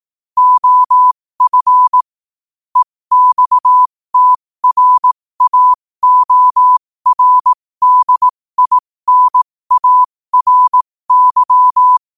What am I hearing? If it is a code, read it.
Morse code, heard as a keyed tone, about 18 wpm, spaced as for 15 wpm, 1000 Hz. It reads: OF EXTRAORDINARY
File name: qotdfiles / 15wpm